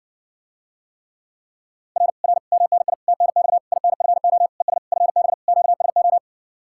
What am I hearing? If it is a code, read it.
Morse code, heard as a keyed tone, about 60 wpm, 700 Hz. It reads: R R MNI TNX INFO ES FB QSO